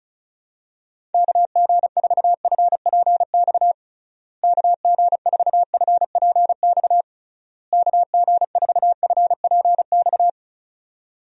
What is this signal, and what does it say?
Morse code, heard as a keyed tone, about 36 wpm, 700 Hz. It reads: KG4FPX KG4FPX KG4FPX
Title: USA Call Signs - 6 Characters - Repeated 3x - 35wpm